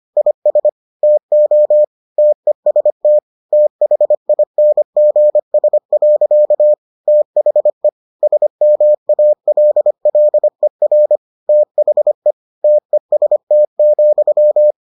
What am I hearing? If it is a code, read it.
Morse code, heard as a keyed tone, about 25 wpm, 600 Hz. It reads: IS TO TEST THINGS. THE SMALLER THE TEST,